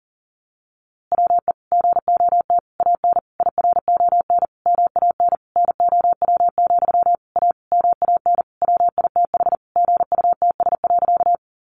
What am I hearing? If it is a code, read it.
Morse code, heard as a keyed tone, about 40 wpm, 700 Hz. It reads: WE GOT AN IRON MAN NOW, A MAN WITH GUTS.
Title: A Sound Of Thunder 40wpm